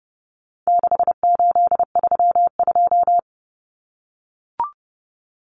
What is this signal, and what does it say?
Morse code, heard as a keyed tone, about 30 wpm, 700 Hz. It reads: 6832
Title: Numbers - 4-Digit 30wpm